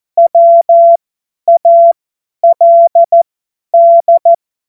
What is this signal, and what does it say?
Morse code, heard as a keyed tone, about 14 wpm, spaced as for 10 wpm, 680 Hz. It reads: WALD
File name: Morsekurs / Witze / Witz 20. Juli 2021, 50 F70